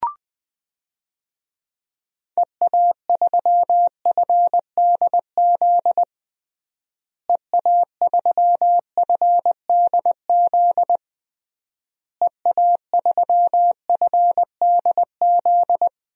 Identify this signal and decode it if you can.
Morse code, heard as a keyed tone, about 21 wpm, 700 Hz. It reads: EA3FDZ EA3FDZ EA3FDZ
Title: DXCC-100 Call Signs - 6 Characters - Repeated 3x - 20wpm